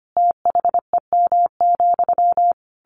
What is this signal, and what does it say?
Morse code, heard as a keyed tone, about 25 wpm, 700 Hz. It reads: THEM,
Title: A Sound Of Thunder 25wpm